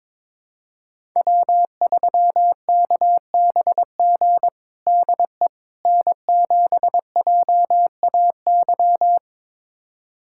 Morse code, heard as a keyed tone, about 22 wpm, 700 Hz. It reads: W3KBG DE N7JAY
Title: USA Callsign DE Callsign - 5 Characters - 22wpm